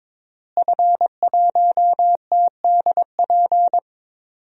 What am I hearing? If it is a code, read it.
Morse code, heard as a keyed tone, about 22 wpm, 700 Hz. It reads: F1TDP
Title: DXCC-100 Call Signs - 5 Characters - Repeated 2x - 22wpm